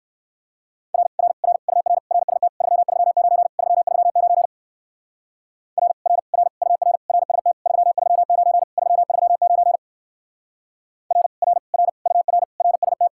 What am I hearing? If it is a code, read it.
Morse code, heard as a keyed tone, about 54 wpm, 700 Hz. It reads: R R R UR RST 339 339 R R R UR RST 339 339 R R R UR RST
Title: Speed-Racing Common QSO Phrases - 45wpm